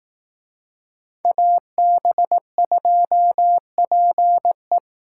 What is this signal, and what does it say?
Morse code, heard as a keyed tone, about 18 wpm, 700 Hz. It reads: AB2PE
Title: Speed Racing - USA Call Signs 15wpm